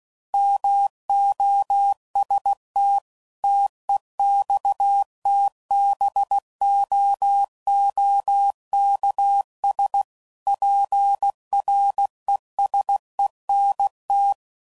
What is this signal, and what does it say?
Morse code, heard as a keyed tone, about 16 wpm, 800 Hz. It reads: MOST TEXTBOOKS PRESENT
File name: CW Code Groups @15WPM / Technical Subjects / Gates Intro